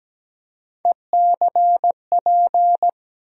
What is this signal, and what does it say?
Morse code, heard as a keyed tone, about 17 wpm, 700 Hz. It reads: ECP
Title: USA Call Sign Postfixes - 3 Letters 17wpm